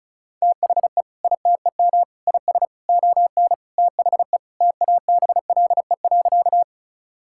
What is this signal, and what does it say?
Morse code, heard as a keyed tone, about 36 wpm, 700 Hz. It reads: THE ITEM IS ON THE TABLE.
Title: Sentences from Top 1000 Words - Review Part 6 - 35wpm